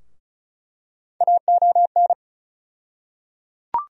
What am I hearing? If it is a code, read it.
Morse code, heard as a keyed tone, about 34 wpm, 700 Hz. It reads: AON
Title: USA Call Sign Postfixes - 3 Letters 35wpm